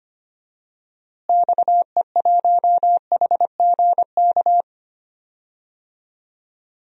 Morse code, heard as a keyed tone, about 25 wpm, 700 Hz. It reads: XE1HGK
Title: DXCC-100 Call Signs - 6 Characters - Repeated 3x - 25wpm